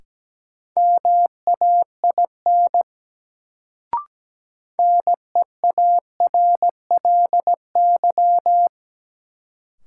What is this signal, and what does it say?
Morse code, heard as a keyed tone, about 17 wpm, 700 Hz. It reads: MAIN NEARLY
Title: Sentences from Top 1300 Words - New Word Drill Alt - 17wpm